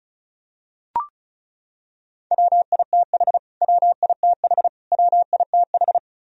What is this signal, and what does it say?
Morse code, heard as a keyed tone, about 36 wpm, 700 Hz. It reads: WITH WITH WITH
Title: Top 100 Words Repeated 35wpm